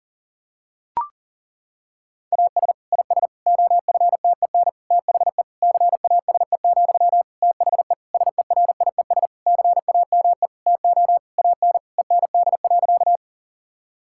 Morse code, heard as a keyed tone, about 40 wpm, 700 Hz. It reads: AS IS OFTEN THE CASE, THE SERIES CAME TO AN END.
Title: Sentences from Top 1000 Words - Review Part 5 - 40wpm